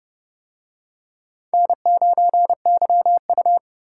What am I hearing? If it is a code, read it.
Morse code, heard as a keyed tone, about 28 wpm, 700 Hz. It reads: N9YU